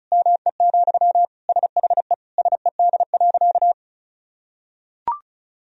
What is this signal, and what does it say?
Morse code, heard as a keyed tone, about 34 wpm, 700 Hz. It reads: ME, SHE SED.
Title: Sentences from Top 500 Words and Abbreviations 35wpm